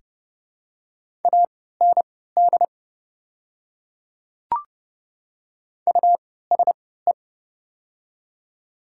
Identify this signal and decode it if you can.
Morse code, heard as a keyed tone, about 30 wpm, spaced as for 17 wpm, 700 Hz. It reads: AND USE
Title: 3x Character Spacing - Top 100 Words - 30wpm